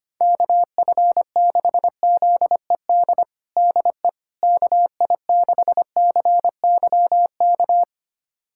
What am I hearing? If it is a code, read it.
Morse code, heard as a keyed tone, about 25 wpm, 700 Hz. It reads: KF6ZED DE KI6CYK
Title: USA Callsign DE Callsign - 6 Characters - 25wpm